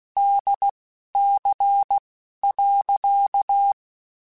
Morse code, heard as a keyed tone, about 16 wpm, spaced as for 11 wpm, 800 Hz. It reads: DC.
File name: Technical Subjects / Technical Subjects-mp3s / Sw Pwr Sup @10